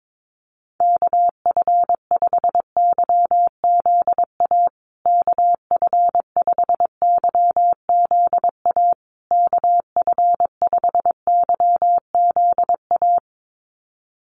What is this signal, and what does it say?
Morse code, heard as a keyed tone, about 22 wpm, 700 Hz. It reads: KF5YZA KF5YZA KF5YZA
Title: USA Call Signs Repeated 3x 22wpm